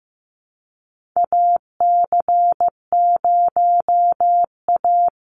Morse code, heard as a keyed tone, about 15 wpm, 700 Hz. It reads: AC0A
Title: NA SOTA Chaser Callsigns 15wpm